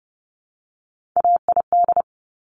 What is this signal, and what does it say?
Morse code, heard as a keyed tone, about 30 wpm, 700 Hz. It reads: AID
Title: Top Three Letter Combinations in Words 30wpm